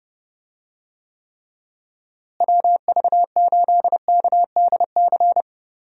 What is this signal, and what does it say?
Morse code, heard as a keyed tone, about 30 wpm, 700 Hz. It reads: WV8KDC
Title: USA Call Signs - 6 Characters - Repeated 2x - 30wpm